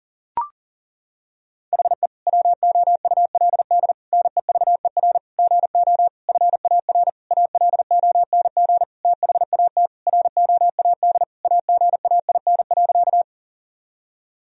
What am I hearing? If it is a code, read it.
Morse code, heard as a keyed tone, about 37 wpm, 700 Hz. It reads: HE WOULD NEVER GO FAR ALONG THAT ROAD AGAIN.
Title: Sentences from Top 1000 Words - Review Part 2 - 40wpm